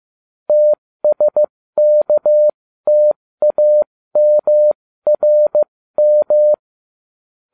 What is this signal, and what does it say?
Morse code, heard as a keyed tone, about 15 wpm, spaced as for 13 wpm, 600 Hz. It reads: TSKTAMRM
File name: cw / ubacw07 1